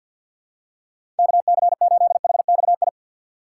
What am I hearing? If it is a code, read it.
Morse code, heard as a keyed tone, about 52 wpm, 700 Hz. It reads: KC8HXI
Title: USA Call Signs - 6 Characters - 4x Character Spacing - 50wpm